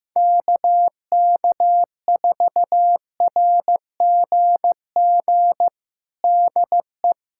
Morse code, heard as a keyed tone, about 15 wpm, 700 Hz. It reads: KK4RGG DE
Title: USA Callsign DE Callsign 15wpm